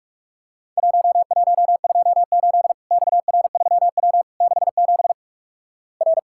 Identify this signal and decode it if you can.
Morse code, heard as a keyed tone, about 48 wpm, 700 Hz. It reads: 1128 XR3W 67 R
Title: Worked All Europe DX Contest - QTC Exchange - 45wpm